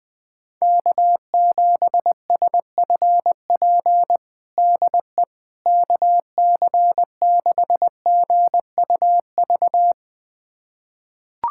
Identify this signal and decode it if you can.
Morse code, heard as a keyed tone, about 21 wpm, 700 Hz. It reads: K7SFP DE KC6GUV
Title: USA Callsign DE Callsign 20wpm